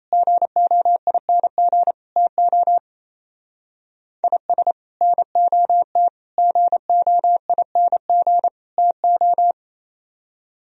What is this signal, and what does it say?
Morse code, heard as a keyed tone, about 34 wpm, 700 Hz. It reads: GOING TO IS NOT GOING TO
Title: Speed-Racing - Sets of 4 Words 28wpm